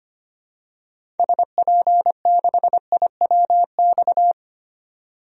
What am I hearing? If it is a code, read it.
Morse code, heard as a keyed tone, about 25 wpm, 700 Hz. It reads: SP6IWX
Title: DXCC-100 Call Signs - 6 Characters - 25wpm